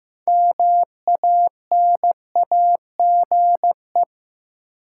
Morse code, heard as a keyed tone, about 15 wpm, 700 Hz. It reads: MANAGE
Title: Sentences from Top 1400 Words - New Word Drill Alt - 15wpm